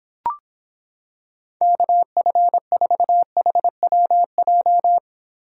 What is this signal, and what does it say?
Morse code, heard as a keyed tone, about 26 wpm, 700 Hz. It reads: KF4HWJ